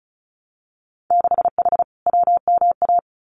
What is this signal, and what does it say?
Morse code, heard as a keyed tone, about 36 wpm, 700 Hz. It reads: 6H WMA
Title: Winter Field Day Exchanges Encore 35wpm